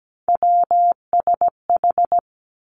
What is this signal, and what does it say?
Morse code, heard as a keyed tone, about 17 wpm, 700 Hz. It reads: WSH
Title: Three Challenging Letter-Number Combinations 17wpm